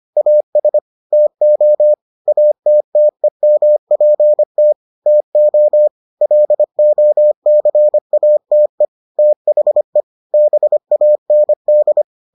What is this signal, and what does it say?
Morse code, heard as a keyed tone, about 25 wpm, 600 Hz. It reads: AS TO ATTEMPT TO LOCATE THE BAND